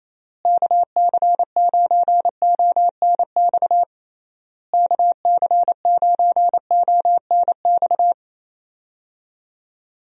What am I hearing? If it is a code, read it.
Morse code, heard as a keyed tone, about 28 wpm, 700 Hz. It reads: KC9ONX KC9ONX
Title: USA Call Signs - 6 Characters - Repeated 3x - 28wpm